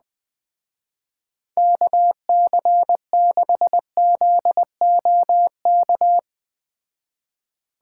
Morse code, heard as a keyed tone, about 20 wpm, 700 Hz. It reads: KC6ZOK